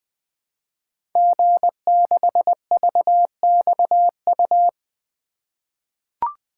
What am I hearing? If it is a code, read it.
Morse code, heard as a keyed tone, about 20 wpm, 700 Hz. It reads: G6VXU
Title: Code Groups - 5 Letter:Numbers - 20wpm